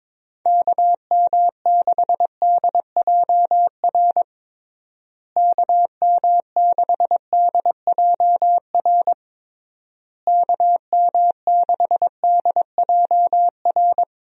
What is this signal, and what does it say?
Morse code, heard as a keyed tone, about 22 wpm, 700 Hz. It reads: KM6DJR KM6DJR KM6DJR
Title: USA Call Signs - 6 Characters - Repeated 3x - 22wpm